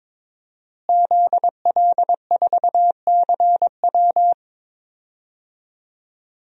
Morse code, heard as a keyed tone, about 22 wpm, 700 Hz. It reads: ZL4CW
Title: DXCC-100 Call Signs - 5 Characters - Repeated 2x - 22wpm